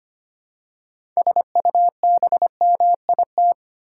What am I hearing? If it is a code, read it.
Morse code, heard as a keyed tone, about 25 wpm, 700 Hz. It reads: SUBMIT
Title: Six Letter Words - 25wpm